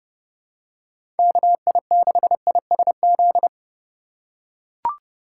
Morse code, heard as a keyed tone, about 32 wpm, 700 Hz. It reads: KI6ISZ
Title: USA Call Signs - 3x Character Spacing - 30wpm